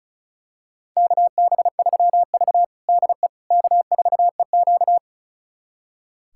Morse code, heard as a keyed tone, about 36 wpm, 690 Hz. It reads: KB3V DE K4EQ
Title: USA Callsign DE Callsign - 4 Characters - 35wpm